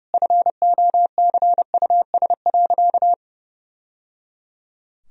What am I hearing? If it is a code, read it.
Morse code, heard as a keyed tone, about 30 wpm, 700 Hz. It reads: FOCUS.
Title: Sentences from Top 1000 Words - Review Part 4 - 30wpm